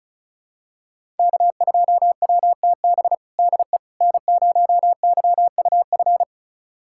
Morse code, heard as a keyed tone, about 34 wpm, 700 Hz. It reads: K2WTB DE N0YUF
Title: USA Callsign DE Callsign - 5 Characters - 35wpm